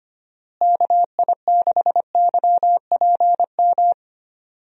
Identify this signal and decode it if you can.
Morse code, heard as a keyed tone, about 25 wpm, 700 Hz. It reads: KI6YPM
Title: USA Call Signs - 6 Characters - Repeated 3x - 25wpm